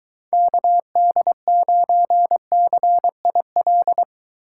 Morse code, heard as a keyed tone, about 23 wpm, 700 Hz. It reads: KD9CIL